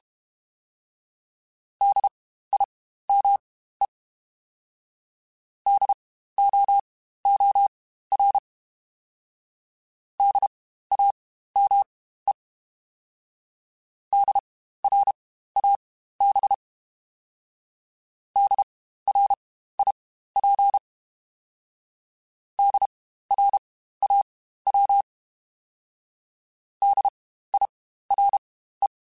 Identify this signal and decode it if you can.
Morse code, heard as a keyed tone, about 32 wpm, spaced as for 15 wpm, 800 Hz. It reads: DIME DOOR DAME DRAB DRIP DRAW DIRE